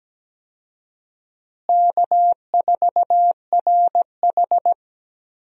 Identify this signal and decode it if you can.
Morse code, heard as a keyed tone, about 17 wpm, 700 Hz. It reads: K4RH
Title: USA Call Signs - 4 Characters - Repeated 2x - 17wpm